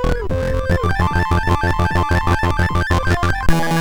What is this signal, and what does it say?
There is no code.